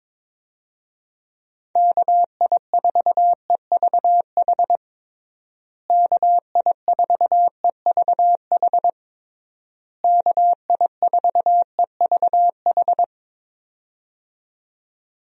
Morse code, heard as a keyed tone, about 21 wpm, 700 Hz. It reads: KI4EVH KI4EVH KI4EVH
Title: USA Call Signs - 6 Characters - Repeated 3x - 22wpm